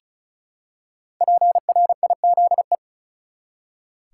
Morse code, heard as a keyed tone, about 34 wpm, 700 Hz. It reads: PRIZE
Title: Five Letter Words - 35wpm